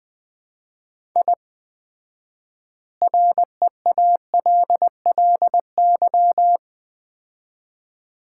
Morse code, heard as a keyed tone, about 21 wpm, 700 Hz. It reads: I REALLY
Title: 4x Word-Spacing - Sets of 4 Words - 20wpm